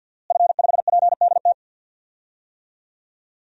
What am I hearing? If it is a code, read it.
Morse code, heard as a keyed tone, about 50 wpm, 700 Hz. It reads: F5PDT
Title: DXCC-100 Call Signs - 5 Characters - Repeated 2x - 50wpm